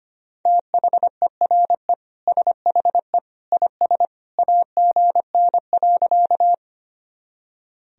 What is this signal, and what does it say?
Morse code, heard as a keyed tone, about 25 wpm, 700 Hz. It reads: THERE SHE IS AGN.
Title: Sentences from Top 200 Words and Abbreviations 25wpm